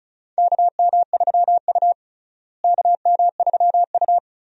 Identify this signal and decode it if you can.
Morse code, heard as a keyed tone, about 34 wpm, 700 Hz. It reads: KM3U KM3U
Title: USA Call Signs - 4 Characters - Repeated 2x - 35wpm